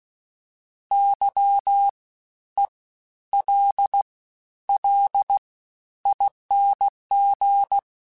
Morse code, heard as a keyed tone, about 16 wpm, 800 Hz. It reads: Y E L L ING
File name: Special INGs-mp3s / Special INGs-mp3s / SlowToFaster ing @15